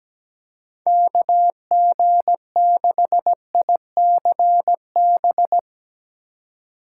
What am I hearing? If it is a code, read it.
Morse code, heard as a keyed tone, about 17 wpm, 700 Hz. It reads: KG6ICB
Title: USA Call Signs - 6 Characters - 4x Character Spacing - 17wpm